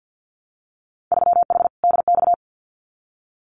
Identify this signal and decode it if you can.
Morse code, heard as a keyed tone, about 50 wpm, 700 Hz. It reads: 3H DX
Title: Winter Field Day Exchanges Encore 50wpm